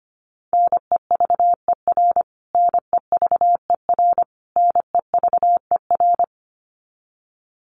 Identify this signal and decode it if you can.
Morse code, heard as a keyed tone, about 25 wpm, 700 Hz. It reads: NEVER NEVER NEVER
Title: Top 500 Words Repeated 25wpm